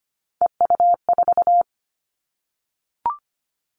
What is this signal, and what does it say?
Morse code, heard as a keyed tone, about 25 wpm, 700 Hz. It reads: EU4
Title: Callsign Prefixes for Top 100 DXCC - 25wpm